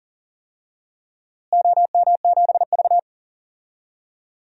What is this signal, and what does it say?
Morse code, heard as a keyed tone, about 40 wpm, 700 Hz. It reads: OM7V
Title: DXCC-100 Call Signs - 4 Characters - Repeated 2x - 40wpm